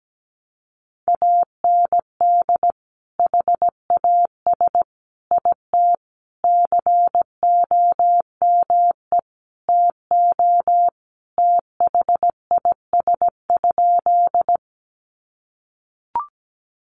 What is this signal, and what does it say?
Morse code, heard as a keyed tone, about 17 wpm, 700 Hz. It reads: AND HAS IT COME TO THIS?
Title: Sentences from Top 100 Words 17wpm